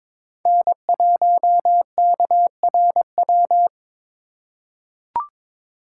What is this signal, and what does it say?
Morse code, heard as a keyed tone, about 22 wpm, 700 Hz. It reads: N1KRW
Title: USA Call Signs - 5 Characters - 3x Character Spacing - 22wpm